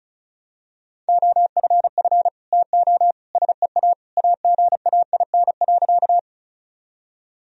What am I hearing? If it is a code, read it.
Morse code, heard as a keyed tone, about 34 wpm, 700 Hz. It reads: OFF TO SEA AGAIN.
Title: Sentences from Top 1000 Words - Review Part 1 - 35wpm